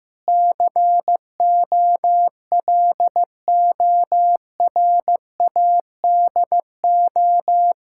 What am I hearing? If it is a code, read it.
Morse code, heard as a keyed tone, about 15 wpm, 700 Hz. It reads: COLORADO
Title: Practice QSO - Exchange 1 - Beginning - 15wpm